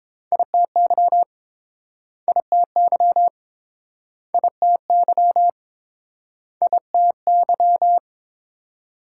Morse code, heard as a keyed tone, about 29 wpm, 700 Hz. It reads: ITY ITY ITY ITY